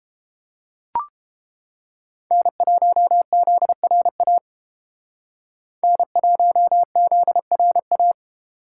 Morse code, heard as a keyed tone, about 30 wpm, 700 Hz. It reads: N1ZRA N1ZRA